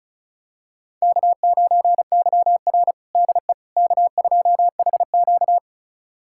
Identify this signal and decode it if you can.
Morse code, heard as a keyed tone, about 34 wpm, 700 Hz. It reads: K9YR DE K2HQ
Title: USA Callsign DE Callsign - 4 Characters - 35wpm